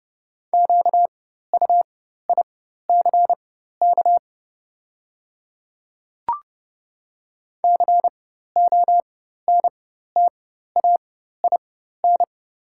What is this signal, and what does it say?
Morse code, heard as a keyed tone, about 30 wpm, spaced as for 14 wpm, 700 Hz. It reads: QUICK CONTAIN
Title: 4x Character Spacing - Top 500 Words - 30wpm